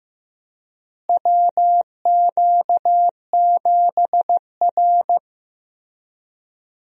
Morse code, heard as a keyed tone, about 15 wpm, 700 Hz. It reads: WQ7R
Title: USA Call Signs - 4 Characters - Repeated 3x - 15wpm